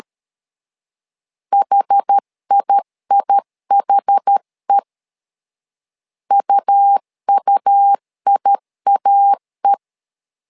Morse code, heard as a keyed tone, about 13 wpm, 800 Hz. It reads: HIIHE UUIAE
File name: LEZIONI / 2 UHDN / 2 UH 12wpm 1439